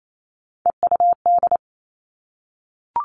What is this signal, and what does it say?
Morse code, heard as a keyed tone, about 28 wpm, 700 Hz. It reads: EUD
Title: Three Challenging Letter-Number Combinations 28wpm